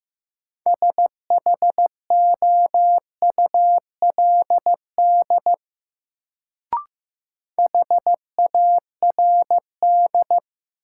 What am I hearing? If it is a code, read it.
Morse code, heard as a keyed tone, about 15 wpm, 700 Hz. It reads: SHOULD HARD
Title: Sentences from Top 300 Words - New Word Drill - 15wpm